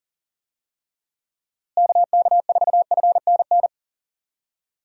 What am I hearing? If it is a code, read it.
Morse code, heard as a keyed tone, about 40 wpm, 700 Hz. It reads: KK4FNN
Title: USA Call Signs - 6 Characters - Repeated 2x - 40wpm